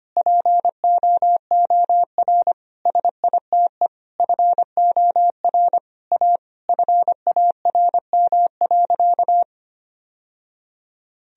Morse code, heard as a keyed tone, about 25 wpm, 700 Hz. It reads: POOR SITE FOR A FARM.
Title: Sentences from Top 1000 Words - Review Part 4 - 25wpm